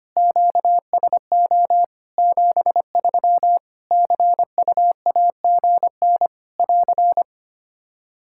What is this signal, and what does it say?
Morse code, heard as a keyed tone, about 25 wpm, 700 Hz. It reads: QSO 73 CUAGN <AR>
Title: Practice QSO - Exchange 4 - Ending - 25wpm